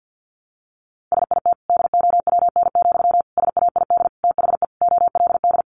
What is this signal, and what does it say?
Morse code, heard as a keyed tone, about 50 wpm, 700 Hz. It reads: SIT DOWN, SAID THE OLD
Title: Sentences from Top 500 Words 50wpm